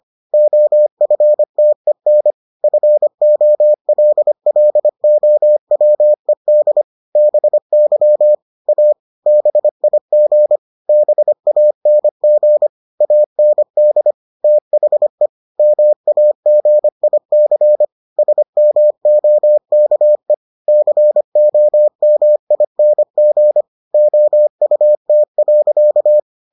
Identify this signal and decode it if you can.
Morse code, heard as a keyed tone, about 25 wpm, 600 Hz. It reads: OFTEN FOLLOWED BY A BIG BANG AND THE MAGIC SMOKE COMING OUT.